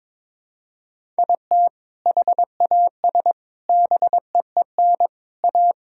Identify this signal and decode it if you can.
Morse code, heard as a keyed tone, about 22 wpm, 700 Hz. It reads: IT HAS BEEN A
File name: practice / 22wpm / Sentences with 2-Word Phrases - 22wpm